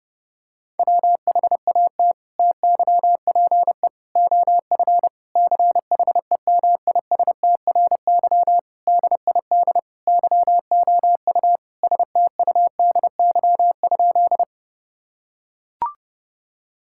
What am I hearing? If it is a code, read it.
Morse code, heard as a keyed tone, about 30 wpm, 700 Hz. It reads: WHAT TYPE OF CHEMISTRY DID YOU STUDY?
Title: Sentences from Top 1000 Words - Review Part 6 - 30wpm